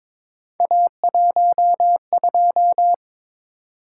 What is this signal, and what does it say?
Morse code, heard as a keyed tone, about 22 wpm, 700 Hz. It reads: A12
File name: practice / 22wpm / Three Letter-Number Combinations - 22wpm